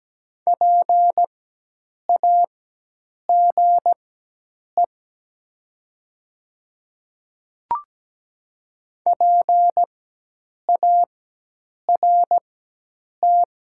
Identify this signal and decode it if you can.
Morse code, heard as a keyed tone, about 17 wpm, spaced as for 8 wpm, 700 Hz. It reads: PAGE PART
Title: 4x Character Spacing - Top 300 Words - 17wpm